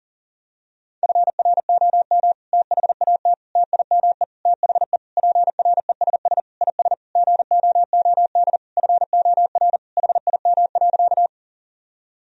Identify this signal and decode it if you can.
Morse code, heard as a keyed tone, about 37 wpm, 700 Hz. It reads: FROM THAT TIME THE PRESS IS GOOD FOR HIM.
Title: Sentences from Top 1000 Words - Review Part 1 - 40wpm